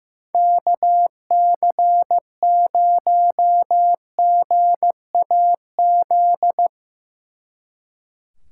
Morse code, heard as a keyed tone, about 16 wpm, 700 Hz. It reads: KC0GAZ
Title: Speed Racing - USA 2x3 Call Signs 15wpm